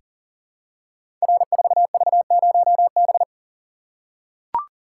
Morse code, heard as a keyed tone, about 38 wpm, 700 Hz. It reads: R4V0B
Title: Code Groups - 5 Letter:Numbers - 40wpm